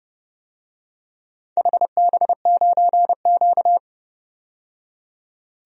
Morse code, heard as a keyed tone, about 29 wpm, 700 Hz. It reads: HB9Q
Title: DXCC-100 Call Signs - 4 Characters - 30wpm